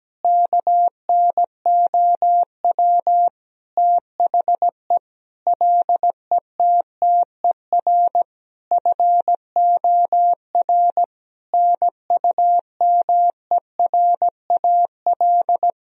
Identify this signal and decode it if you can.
Morse code, heard as a keyed tone, about 17 wpm, 700 Hz. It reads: KNOW THE LETTER FOR NUMERAL
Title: Sentences from Top 1000 Words - Review Part 2 - 17wpm